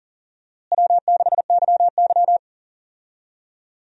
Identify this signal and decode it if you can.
Morse code, heard as a keyed tone, about 40 wpm, 700 Hz. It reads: W6YY
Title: USA Call Signs - 4 Characters - Repeated 2x - 40wpm